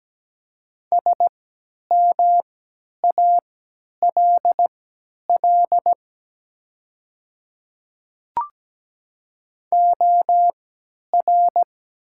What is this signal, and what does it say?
Morse code, heard as a keyed tone, about 17 wpm, spaced as for 10 wpm, 700 Hz. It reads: SMALL OR
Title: 3x Character Spacing - Top 200 Words - 17wpm